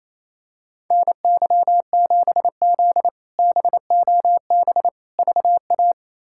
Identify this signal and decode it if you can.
Morse code, heard as a keyed tone, about 28 wpm, 700 Hz. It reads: NY7Z BOB VA
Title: K1USN SST QSO Practice - 28wpm